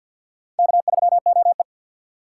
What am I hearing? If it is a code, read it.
Morse code, heard as a keyed tone, about 61 wpm, 700 Hz. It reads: K3OE
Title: Speed Racing - USA Call Signs 50wpm